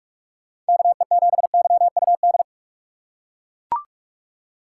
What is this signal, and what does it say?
Morse code, heard as a keyed tone, about 41 wpm, 700 Hz. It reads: KE7YUD